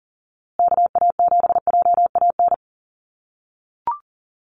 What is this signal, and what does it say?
Morse code, heard as a keyed tone, about 40 wpm, 700 Hz. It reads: KA7JAN
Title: USA Call Signs Repeated 2x 40wpm